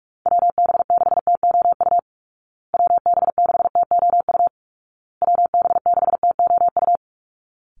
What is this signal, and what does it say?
Morse code, heard as a keyed tone, about 44 wpm, 700 Hz. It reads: WB6TOU WB6TOU WB6TOU
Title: Call Signs Repeated 2x - 45wpm